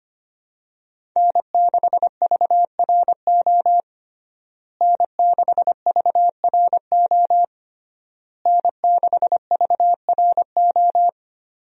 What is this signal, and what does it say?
Morse code, heard as a keyed tone, about 25 wpm, 700 Hz. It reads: N6VRO N6VRO N6VRO
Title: USA Call Signs - 5 Characters - Repeated 3x - 25wpm